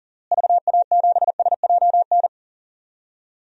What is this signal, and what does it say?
Morse code, heard as a keyed tone, about 37 wpm, 700 Hz. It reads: VA7SJN
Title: DXCC-100 Call Signs - 6 Characters - Repeated 3x - 40wpm